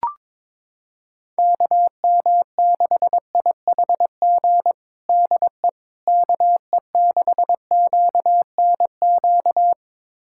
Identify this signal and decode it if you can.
Morse code, heard as a keyed tone, about 21 wpm, 700 Hz. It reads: KM6IHG DE KE6QNQ
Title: USA Callsign DE Callsign - 6 Characters - 22wpm